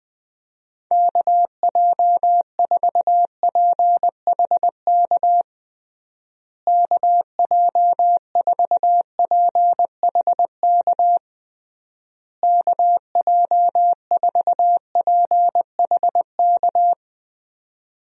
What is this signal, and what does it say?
Morse code, heard as a keyed tone, about 20 wpm, 700 Hz. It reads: KJ4PHK KJ4PHK KJ4PHK
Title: USA Call Signs - 6 Characters - Repeated 3x - 20wpm